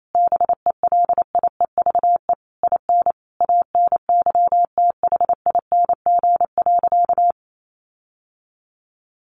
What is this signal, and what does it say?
Morse code, heard as a keyed tone, about 28 wpm, 700 Hz. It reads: BELIEVE IN ANYTHING.
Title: Sentences from Top 1500 Words 28wpm